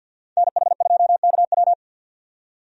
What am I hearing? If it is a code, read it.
Morse code, heard as a keyed tone, about 50 wpm, 700 Hz. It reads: NH2KW
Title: USA Call Signs - 5 Characters - 50wpm